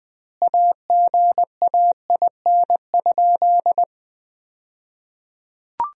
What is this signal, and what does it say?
Morse code, heard as a keyed tone, about 20 wpm, 700 Hz. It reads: AGAIN?
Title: 2x Word-Spacing - Sentences from Top 200 Words - 20wpm